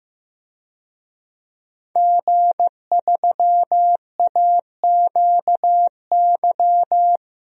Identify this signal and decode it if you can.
Morse code, heard as a keyed tone, about 15 wpm, 700 Hz. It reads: G3AQY
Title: DXCC-100 Call Signs - 5 Characters - 15wpm